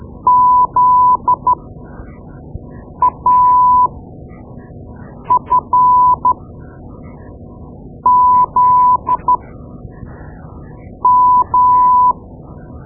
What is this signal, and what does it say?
Morse code, heard as a keyed tone, about 13 wpm, spaced as for 5 wpm, 1000 Hz. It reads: ZAFZM
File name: Morse Lessons / Morse Lessons / Tape-2B